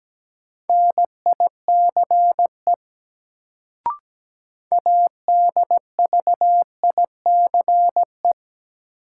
Morse code, heard as a keyed tone, about 17 wpm, 700 Hz. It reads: NICE ADVICE
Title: Sentences from Top 1100 Words - New Word Drill - 17wpm